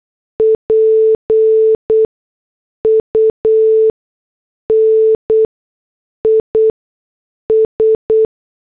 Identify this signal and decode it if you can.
Morse code, heard as a keyed tone, about 8 wpm, 440 Hz. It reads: P U N I S
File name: morse / headlines